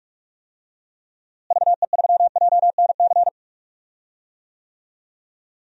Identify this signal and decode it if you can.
Morse code, heard as a keyed tone, about 44 wpm, 700 Hz. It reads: VE3JNC
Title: DXCC-100 Call Signs - 6 Characters - Repeated 2x - 45wpm